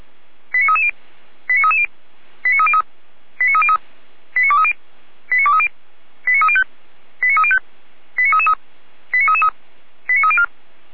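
Five-tone selective call (ZVEI), heard as quick runs of five tones, tones of about 80 ms. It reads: 89330 89330 89303 89303 89230 89230 89305 89305 89333 89304
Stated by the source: ZVEI five-tone call received by SDR, archived by Feuerwehr Orlamünde.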